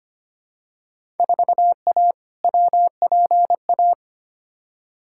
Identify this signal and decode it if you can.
Morse code, heard as a keyed tone, about 25 wpm, 700 Hz. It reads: 4A WPA
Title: Field Day Exchanges Encore - 25wpm